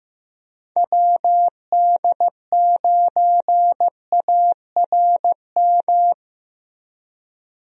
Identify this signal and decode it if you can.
Morse code, heard as a keyed tone, about 15 wpm, 700 Hz. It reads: WD9ARM
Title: USA Call Signs - 6 Characters - Repeated 2x - 15wpm